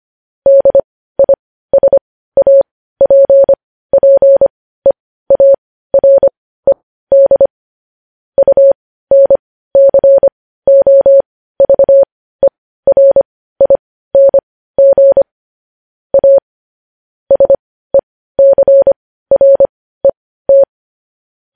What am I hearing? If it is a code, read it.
Morse code, heard as a keyed tone, about 25 wpm, spaced as for 15 wpm, 550 Hz. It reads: DISAPPEARED UNCOVERING A SECRET
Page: News Headlines in Morse Code at 15 WPM